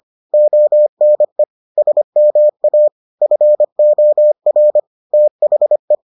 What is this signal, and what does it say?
Morse code, heard as a keyed tone, about 25 wpm, 600 Hz. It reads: ONE SMA FOR THE